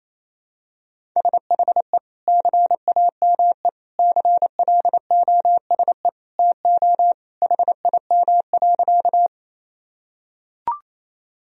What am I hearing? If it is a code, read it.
Morse code, heard as a keyed tone, about 27 wpm, 700 Hz. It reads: SHE CAME CLOSE TO HIM.
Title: Sentences from Top 1000 Words - Review Part 1 - 28wpm